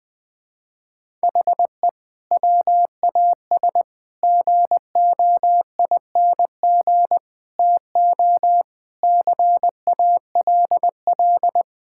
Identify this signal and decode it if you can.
Morse code, heard as a keyed tone, about 20 wpm, 700 Hz. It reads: HE WAS GOING TO CALL
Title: Sentences with 4-Word Phrases - 20wpm